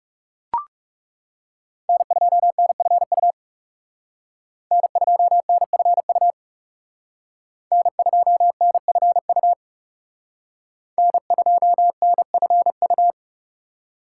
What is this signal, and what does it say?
Morse code, heard as a keyed tone, about 38 wpm, 700 Hz. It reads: N2NFU N2NFU N2NFU N2NFU